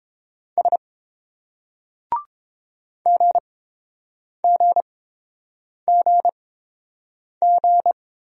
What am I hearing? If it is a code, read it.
Morse code, heard as a keyed tone, about 29 wpm, spaced as for 7 wpm, 700 Hz. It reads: S GGGG